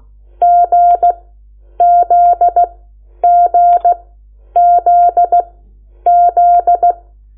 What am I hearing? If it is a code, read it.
Morse code, heard as a keyed tone, about 16 wpm, spaced as for 9 wpm, 700 Hz. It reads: GZGZZ
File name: Lecciones / CLASE Nº8